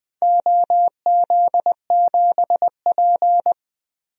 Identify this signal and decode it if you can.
Morse code, heard as a keyed tone, about 20 wpm, 700 Hz. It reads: OZ7P
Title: DXCC-100 Call Signs - 4 Characters - 20wpm